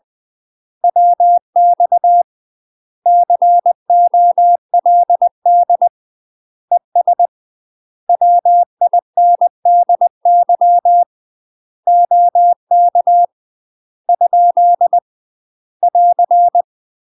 Morse code, heard as a keyed tone, about 20 wpm, 700 Hz. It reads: WX COLD ES WINDY OK ? <AR>
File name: UK MASTER QSOs TRAINING / 20WPM / UK MASTER QSOs 0048